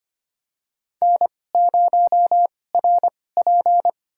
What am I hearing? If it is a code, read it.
Morse code, heard as a keyed tone, about 25 wpm, spaced as for 18 wpm, 700 Hz. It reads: N0RP
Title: USA Call Signs - 2x Character Spacing - 25wpm